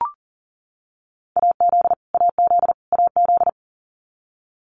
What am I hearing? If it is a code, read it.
Morse code, heard as a keyed tone, about 40 wpm, 700 Hz. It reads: AZ AZ AZ
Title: US States Repeated 3x 40wpm